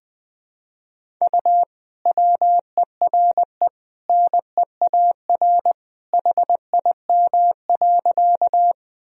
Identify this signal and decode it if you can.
Morse code, heard as a keyed tone, about 20 wpm, 700 Hz. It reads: U WERE NEAR HIM.
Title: Sentences from Top 200 Words and Abbreviations 20wpm